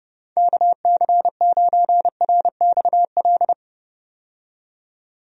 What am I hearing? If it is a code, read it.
Morse code, heard as a keyed tone, about 30 wpm, 700 Hz. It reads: KC9RXL
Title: USA Call Signs - 6 Characters - 30wpm